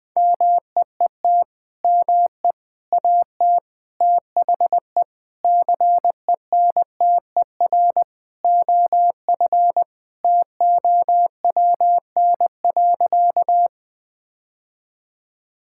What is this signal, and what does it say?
Morse code, heard as a keyed tone, about 20 wpm, 700 Hz. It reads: MEET ME AT THE CENTER OF TOWN.
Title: Sentences with 3-Word Phrases - Spoken Beginning - 20wpm